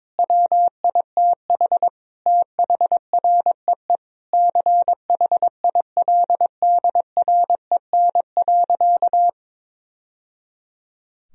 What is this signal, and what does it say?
Morse code, heard as a keyed tone, about 22 wpm, 700 Hz. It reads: WITH THREE CHILDREN.
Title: Sentences from Top 1000 Words - Review Part 1 - 22wpm